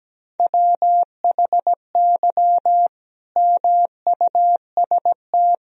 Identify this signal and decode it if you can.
Morse code, heard as a keyed tone, about 17 wpm, 700 Hz. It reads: WHY MUST
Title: Sentences from Top 1000 Words - Review Part 2 - 17wpm